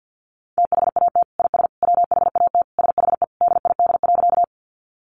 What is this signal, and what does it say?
Morse code, heard as a keyed tone, about 50 wpm, 700 Hz. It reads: THAT IS WHAT SHE DID.
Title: Sentences from Top 100 Words 50wpm